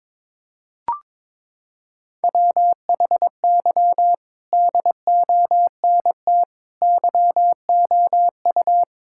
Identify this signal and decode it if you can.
Morse code, heard as a keyed tone, about 22 wpm, 700 Hz. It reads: WHY DONT YOU
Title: Sentences from Top 1000 Words - Review Part 5 - 22wpm